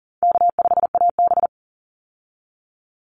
Morse code, heard as a keyed tone, about 40 wpm, 700 Hz. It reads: K5AB
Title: Easy Call Signs Repeated 3x - 40wpm